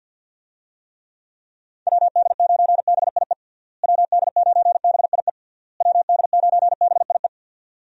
Morse code, heard as a keyed tone, about 48 wpm, 700 Hz. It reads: WD9BIE WD9BIE WD9BIE
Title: USA Call Signs - 6 Characters - Repeated 3x - 50wpm